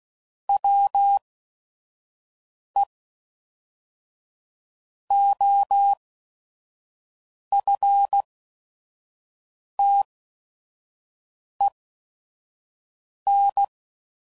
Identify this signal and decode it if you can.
Morse code, heard as a keyed tone, about 16 wpm, spaced as for 5 wpm, 800 Hz. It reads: WE OFTEN
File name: CW Code Groups @5WPM